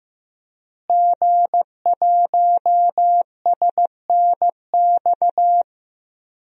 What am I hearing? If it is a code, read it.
Morse code, heard as a keyed tone, about 15 wpm, 700 Hz. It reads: G1SNX
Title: DXCC-100 Call Signs - 5 Characters - Repeated 3x - 15wpm